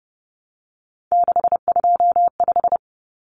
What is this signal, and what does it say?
Morse code, heard as a keyed tone, about 30 wpm, 700 Hz. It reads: 625
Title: Numbers - 3-Digit 30wpm